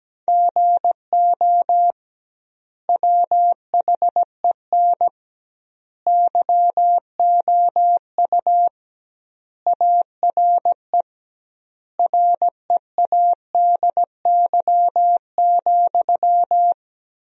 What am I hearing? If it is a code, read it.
Morse code, heard as a keyed tone, about 17 wpm, 700 Hz. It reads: GO WHEN YOU ARE READY,
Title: 2x Word-Spacing - Sentences from Top 500 Words - 17wpm